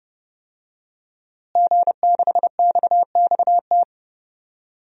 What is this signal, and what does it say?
Morse code, heard as a keyed tone, about 30 wpm, 700 Hz. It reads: G6XXT
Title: DXCC-100 Call Signs - 5 Characters - Repeated 2x - 30wpm